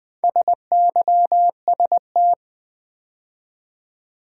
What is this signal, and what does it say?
Morse code, heard as a keyed tone, about 20 wpm, 690 Hz. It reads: SYST
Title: Build A Word - 2 Letters At A Time - 6-Letter Words - 20wpm